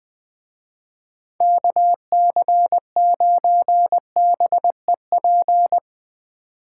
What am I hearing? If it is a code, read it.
Morse code, heard as a keyed tone, about 21 wpm, 700 Hz. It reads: KC9BEP
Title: Speed Racing - USA Call Signs 17wpm